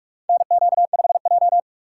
Morse code, heard as a keyed tone, about 44 wpm, 700 Hz. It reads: NQ5J
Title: USA Call Signs - 2x Character Spacing - 45wpm